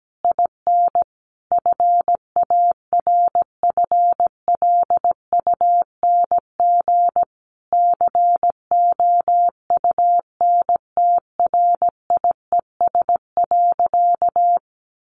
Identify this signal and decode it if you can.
Morse code, heard as a keyed tone, about 17 wpm, 700 Hz. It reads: IN FARFLUNG COUNTRIES.
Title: A Sound Of Thunder 17wpm